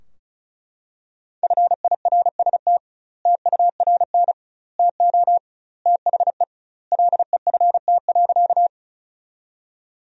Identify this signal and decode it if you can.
Morse code, heard as a keyed tone, about 36 wpm, 700 Hz. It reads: FIRST TURN TO THE LEFT.
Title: 2x Word-Spacing - Sentences from Top 300 Words - 35wpm